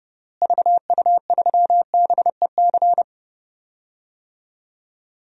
Morse code, heard as a keyed tone, about 31 wpm, 700 Hz. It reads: VU3BEC
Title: DXCC-100 Call Signs - 6 Characters - Repeated 2x - 30wpm